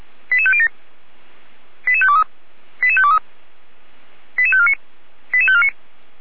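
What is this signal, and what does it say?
Five-tone selective call (ZVEI), heard as quick runs of five tones, tones of about 70 ms. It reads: 88507 88523 88523 88549 88549
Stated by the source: ZVEI five-tone call 88505 received on SDR.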